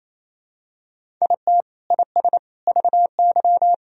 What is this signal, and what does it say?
Morse code, heard as a keyed tone, about 27 wpm, 700 Hz. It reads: IT IS VY
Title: Sentences from Top 500 Words and Abbreviations 28wpm